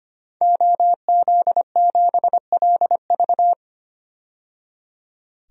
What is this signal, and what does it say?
Morse code, heard as a keyed tone, about 25 wpm, 700 Hz. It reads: OZ7LV
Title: DXCC-100 Call Signs - 5 Characters - 25wpm